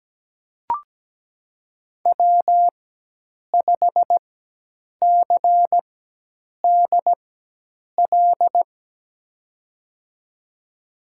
Morse code, heard as a keyed tone, about 17 wpm, spaced as for 8 wpm, 700 Hz. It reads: W5CDL
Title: USA Call Signs - 5 Characters - 4x Character Spacing - 17wpm